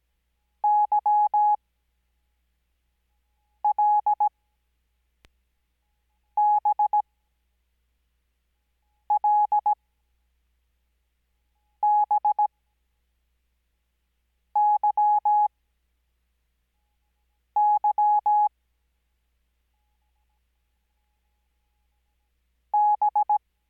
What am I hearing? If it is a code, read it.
Morse code, heard as a keyed tone, about 17 wpm, spaced as for 4 wpm, 840 Hz. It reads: YLBLBYY B
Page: Farnsworth Spacing: 15WPM Character speed Spaced at 5WPM.